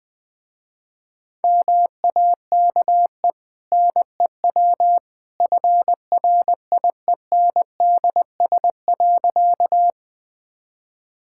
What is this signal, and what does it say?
Morse code, heard as a keyed tone, about 20 wpm, 700 Hz. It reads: MAKE NEW FRIENDS.
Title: Easy Sentences with Hints - 20wpm